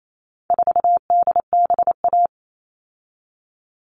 Morse code, heard as a keyed tone, about 28 wpm, 700 Hz. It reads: 4DBA
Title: All Cut Numbers - 4-Digit 28wpm